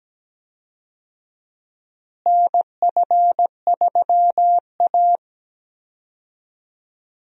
Morse code, heard as a keyed tone, about 17 wpm, 700 Hz. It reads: NF3A
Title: USA Call Signs - 4 Characters - Repeated 2x - 17wpm